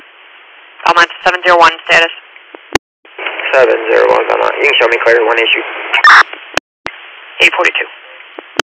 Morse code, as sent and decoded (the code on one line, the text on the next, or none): none